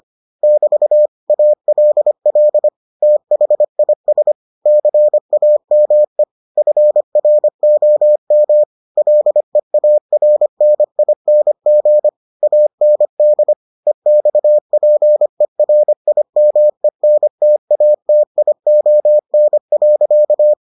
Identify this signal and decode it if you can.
Morse code, heard as a keyed tone, about 25 wpm, 600 Hz. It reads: <BT> ALL THIS CAME FROM LEARNING AND EXPERIMENTATION.